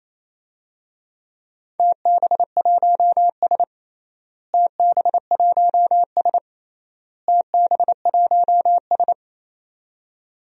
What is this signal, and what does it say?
Morse code, heard as a keyed tone, about 28 wpm, 700 Hz. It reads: TB1S TB1S TB1S
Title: DXCC-100 Call Signs - 4 Characters - Repeated 3x - 28wpm